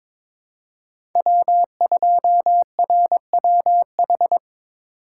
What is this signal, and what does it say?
Morse code, heard as a keyed tone, about 22 wpm, 700 Hz. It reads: W2RWH
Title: USA Call Signs - 5 Characters - 4x Character Spacing - 22wpm